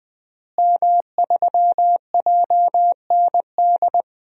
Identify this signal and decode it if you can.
Morse code, heard as a keyed tone, about 20 wpm, 700 Hz. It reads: M3JND
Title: DXCC-100 Call Signs - 5 Characters - Repeated 3x - 20wpm